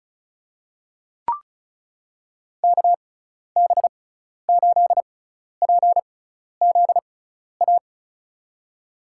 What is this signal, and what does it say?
Morse code, heard as a keyed tone, about 36 wpm, spaced as for 12 wpm, 690 Hz. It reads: KB8PZA
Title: USA Call Signs - 6 Characters - 6x Character Spacing - 35wpm